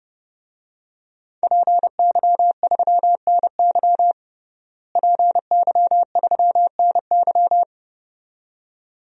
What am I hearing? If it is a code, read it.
Morse code, heard as a keyed tone, about 30 wpm, 700 Hz. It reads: PY3NY PY3NY
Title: DXCC-100 Call Signs - 5 Characters - Repeated 2x - 30wpm